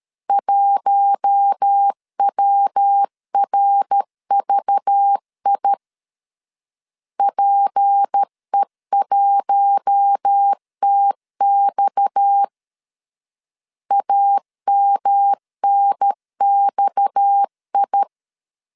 Morse code, heard as a keyed tone, about 13 wpm, 800 Hz. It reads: 1WRVI PE1TX AMNXI
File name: LEZIONI / 6 PKX0 / 6 PKX 12wpm 2024